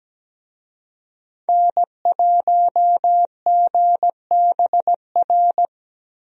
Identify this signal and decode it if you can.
Morse code, heard as a keyed tone, about 17 wpm, 700 Hz. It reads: N1GBR
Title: USA Call Signs - 5 Characters - Repeated 3x - 17wpm